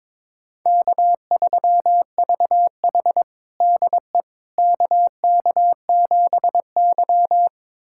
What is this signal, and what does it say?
Morse code, heard as a keyed tone, about 22 wpm, 700 Hz. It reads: K3VH DE KK7Y
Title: USA Callsign DE Callsign - 4 Characters - 22wpm